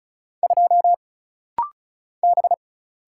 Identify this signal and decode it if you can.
Morse code, heard as a keyed tone, about 36 wpm, 700 Hz. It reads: 2 B
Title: Single Letter-Number - Rapid-Fire 35wpm